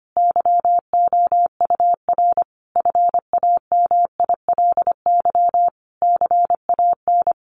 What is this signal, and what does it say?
Morse code, heard as a keyed tone, about 25 wpm, 700 Hz. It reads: YOUR FAMILY CAN
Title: Sentences from Top 1300 Words 25wpm